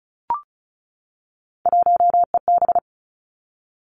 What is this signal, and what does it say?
Morse code, heard as a keyed tone, about 34 wpm, 700 Hz. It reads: 1EB
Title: Three Challenging Letter-Number Combinations 35wpm